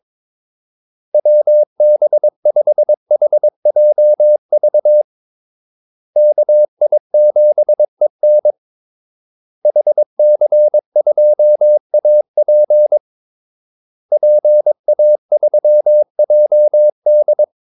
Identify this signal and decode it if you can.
Morse code, heard as a keyed tone, about 22 wpm, 600 Hz. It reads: WB5HJV KI7EN HC2AP PA3JD
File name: Callsigns-22wpm